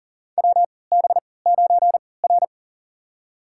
Morse code, heard as a keyed tone, about 40 wpm, spaced as for 23 wpm, 700 Hz. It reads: WB9R
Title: USA Call Signs - 4 Characters - 3x Character Spacing - 40wpm